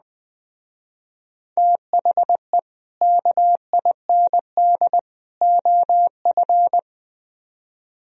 Morse code, heard as a keyed tone, about 20 wpm, 700 Hz. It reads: THE KIND OF
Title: Speed-Racing - Sets of 3 Words 17wpm